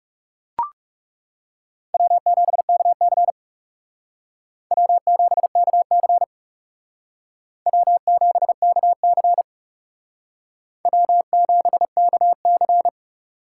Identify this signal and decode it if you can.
Morse code, heard as a keyed tone, about 36 wpm, 700 Hz. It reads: W7KC W7KC W7KC W7KC
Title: Speed Racing - USA 1x2 Call Signs 30wpm